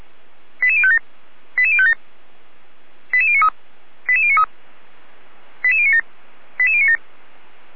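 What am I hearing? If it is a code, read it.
Five-tone selective call (ZVEI), heard as quick runs of five tones, tones of about 70 ms. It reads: 88057 88057 88093 88093 88097 88097
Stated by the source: Five-tone ZVEI selective call received on an SDR receiver, recorded by the fire brigade Feuerwehr Orlamünde.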